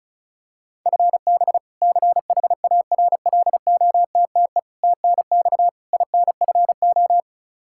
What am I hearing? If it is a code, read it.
Morse code, heard as a keyed tone, about 34 wpm, 700 Hz. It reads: FB CHARLOTTE TNX INFO
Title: Practice QSO - Exchange 3 - More Info - 35wpm